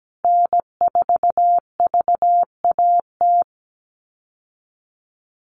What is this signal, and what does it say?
Morse code, heard as a keyed tone, about 17 wpm, 700 Hz. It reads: N4VAT
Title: USA Call Signs Repeated 3x 17wpm